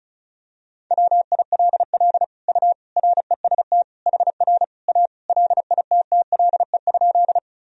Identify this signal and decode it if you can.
Morse code, heard as a keyed tone, about 36 wpm, 700 Hz. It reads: WILL U REST HR A LITTLE?
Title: Sentences from Top 500 Words and Abbreviations 35wpm